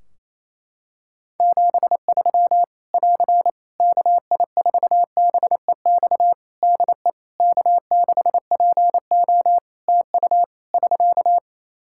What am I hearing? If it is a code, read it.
Morse code, heard as a keyed tone, about 27 wpm, 700 Hz. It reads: 73 <AR> KI4BEX DE K6PO TU <SK>
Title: Practice QSO - Exchange 4 - Ending - 28wpm